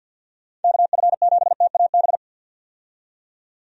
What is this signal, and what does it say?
Morse code, heard as a keyed tone, about 52 wpm, 700 Hz. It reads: KF7TAB
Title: USA Call Signs - 6 Characters - 50wpm